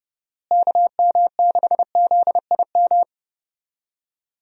Morse code, heard as a keyed tone, about 30 wpm, 700 Hz. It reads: KM6ZIM
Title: USA Call Signs - 6 Characters - Repeated 2x - 30wpm